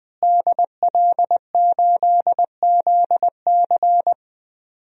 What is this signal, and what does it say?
Morse code, heard as a keyed tone, about 20 wpm, 700 Hz. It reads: DL8ZC
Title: DXCC-100 Call Signs - 5 Characters - Repeated 3x - 20wpm